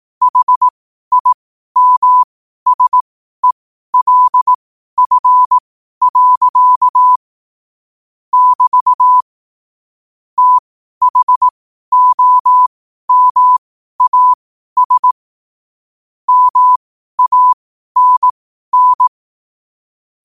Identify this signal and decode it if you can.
Morse code, heard as a keyed tone, about 18 wpm, spaced as for 12 wpm, 1000 Hz. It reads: HIMSELF. <BT> THOMAS MANN
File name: qotdfiles / 12wpm